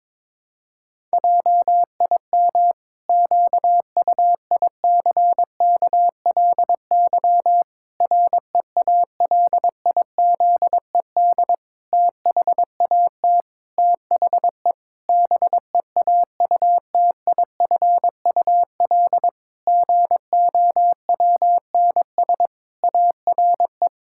Morse code, heard as a keyed tone, about 22 wpm, 700 Hz. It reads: JIM QUICKLY REALIZED THAT THE BEAUTIFUL GOWNS ARE
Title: Pangram Challenge v2 - 22wpm